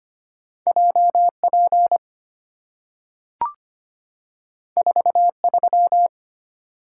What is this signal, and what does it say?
Morse code, heard as a keyed tone, about 25 wpm, 700 Hz. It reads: JP 43
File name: Two Letter-Number Combinations - 25wpm